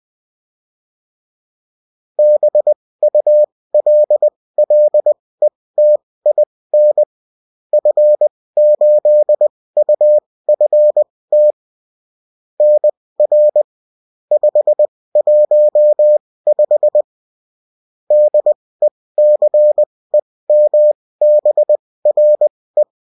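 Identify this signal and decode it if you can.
Morse code, heard as a keyed tone, about 20 wpm, spaced as for 16 wpm, 600 Hz. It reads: BULLETIN F8UFT NR 515 DECEMBRE
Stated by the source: Bulletin in text and CW form.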